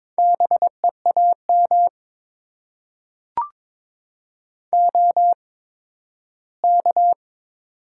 Morse code, heard as a keyed tone, about 22 wpm, spaced as for 13 wpm, 700 Hz. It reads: BEAM OK
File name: Top 90 QSO Elements - 8x Character Spacing - 22wpm